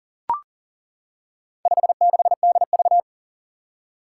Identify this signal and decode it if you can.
Morse code, heard as a keyed tone, about 38 wpm, 700 Hz. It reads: 56DV
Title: Four Challenging Letter-Number Combinations 40wpm